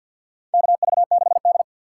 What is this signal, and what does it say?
Morse code, heard as a keyed tone, about 50 wpm, 700 Hz. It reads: KV6D
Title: USA Call Signs - 4x Character Spacing - 50wpm